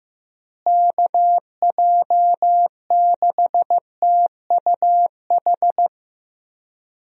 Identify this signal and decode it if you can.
Morse code, heard as a keyed tone, about 15 wpm, 700 Hz. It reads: KJ6TUH
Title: USA Call Signs - 6 Characters - Repeated 3x - 15wpm